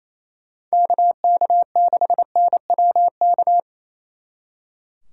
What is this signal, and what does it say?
Morse code, heard as a keyed tone, about 33 wpm, 700 Hz. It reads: KK6NWK
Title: Speed Racing - USA 2x3 Call Signs 28wpm